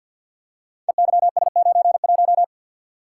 Morse code, heard as a keyed tone, about 47 wpm, 700 Hz. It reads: EYS91
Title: Code Groups - 5 Letter:Numbers - 50wpm